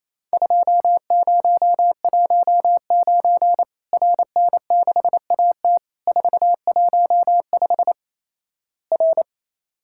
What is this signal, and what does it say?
Morse code, heard as a keyed tone, about 28 wpm, 700 Hz. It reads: 2019 RN6AT 415 R
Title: Worked All Europe DX Contest - QTC Exchange - 28wpm